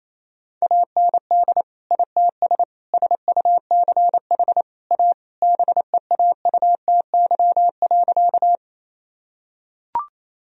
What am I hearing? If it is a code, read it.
Morse code, heard as a keyed tone, about 28 wpm, 700 Hz. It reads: AND ITS SUCH A BEAUTY.
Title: Sentences from Top 1000 Words - Review Part 2 - 28wpm